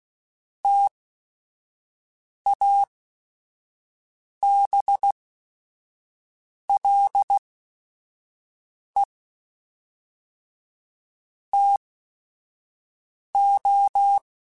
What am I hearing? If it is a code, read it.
Morse code, heard as a keyed tone, about 16 wpm, spaced as for 5 wpm, 800 Hz. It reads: TABLE TO
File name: Technical Subjects